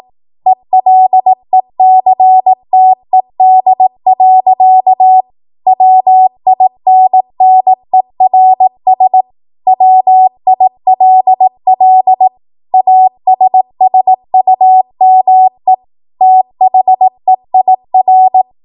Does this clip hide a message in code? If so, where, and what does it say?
Yes: Morse code from 0.5 to 18.4 s, heard as a keyed tone, about 18 wpm, 750 Hz. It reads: ELECTED. WINNERS WILL ASSUME THEIR